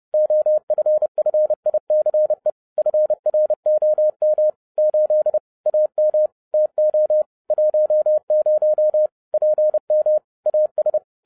Morse code, heard as a keyed tone, about 30 wpm, 610 Hz. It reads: OFFICE FROM 8 AM TO 10 PM AS
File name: 2011-07-mp3 / n60 BACK FROM A BIG ACCIDENT 30wpm